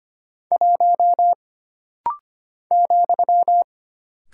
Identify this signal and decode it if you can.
Morse code, heard as a keyed tone, about 25 wpm, 700 Hz. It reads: 1 ,
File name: Single Character (Letter:Number:Punctuation:Prosign) - Rapid-Fire - 25wpm